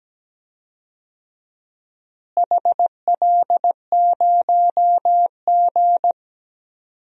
Morse code, heard as a keyed tone, about 17 wpm, 700 Hz. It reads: HL0G
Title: DXCC-100 Call Signs - 4 Characters - Repeated 3x - 17wpm